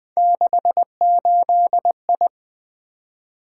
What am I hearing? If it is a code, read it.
Morse code, heard as a keyed tone, about 20 wpm, 700 Hz. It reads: 68I
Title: Three Letter-Number Combinations - 20wpm